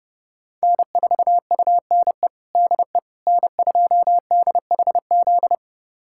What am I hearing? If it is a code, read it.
Morse code, heard as a keyed tone, about 30 wpm, 700 Hz. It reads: N4UNE DE N2DHZ
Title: USA Callsign DE Callsign - 5 Characters - 30wpm